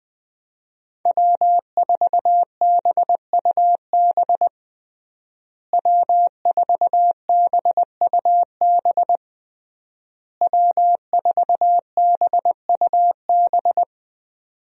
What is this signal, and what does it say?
Morse code, heard as a keyed tone, about 21 wpm, 700 Hz. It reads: W4BUB W4BUB W4BUB
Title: USA Call Signs - 5 Characters - Repeated 3x - 20wpm